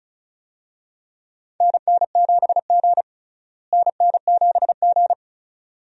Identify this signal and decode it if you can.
Morse code, heard as a keyed tone, about 36 wpm, 700 Hz. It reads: NN7G NN7G
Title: USA Call Signs - 4 Characters - Repeated 2x - 35wpm